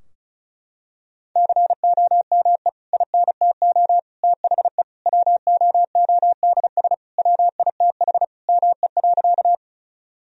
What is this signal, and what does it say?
Morse code, heard as a keyed tone, about 34 wpm, 700 Hz. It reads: COME INTO THE WOODS WITH ME.
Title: Sentences from Top 1000 Words - Review Part 2 - 35wpm